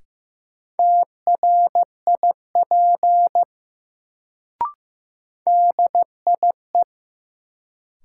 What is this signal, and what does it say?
Morse code, heard as a keyed tone, about 15 wpm, 700 Hz. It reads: TRIP DIE
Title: Sentences from Top 1300 Words - New Word Drill Alt - 15wpm